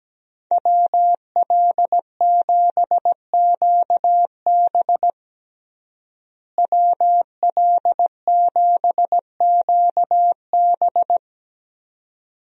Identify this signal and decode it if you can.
Morse code, heard as a keyed tone, about 17 wpm, 700 Hz. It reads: WL7QB WL7QB
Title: USA Call Signs - 5 Characters - Repeated 2x - 17wpm